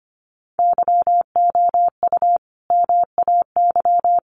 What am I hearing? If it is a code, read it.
Morse code, heard as a keyed tone, about 25 wpm, 700 Hz. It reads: YOU MAY
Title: Sentences from Top 700 Words 25wpm